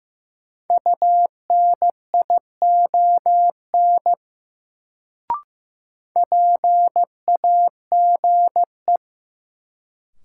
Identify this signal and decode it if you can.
Morse code, heard as a keyed tone, about 15 wpm, 700 Hz. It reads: UNION PAGE
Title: Sets of 5 Words Encore - New Word Drill - 15wpm